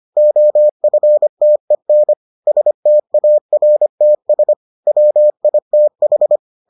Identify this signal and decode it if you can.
Morse code, heard as a keyed tone, about 25 wpm, 600 Hz. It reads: OFTEN STARTS WITH